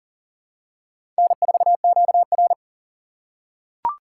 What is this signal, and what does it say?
Morse code, heard as a keyed tone, about 38 wpm, 700 Hz. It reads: N4QR
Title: USA Call Signs - 4 Characters - 3x Character Spacing - 40wpm